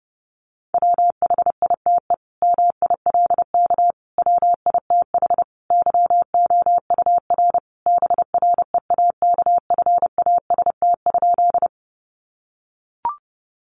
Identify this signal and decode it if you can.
Morse code, heard as a keyed tone, about 30 wpm, 700 Hz. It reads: WHITE MILK WITH YOUR BREAKFAST?
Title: Sentences from Top 1500 Words 30wpm